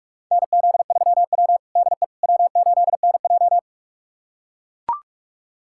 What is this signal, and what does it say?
Morse code, heard as a keyed tone, about 44 wpm, 700 Hz. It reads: NZ3W DE W8NJ
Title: USA Callsign DE Callsign - 4 Characters - 45wpm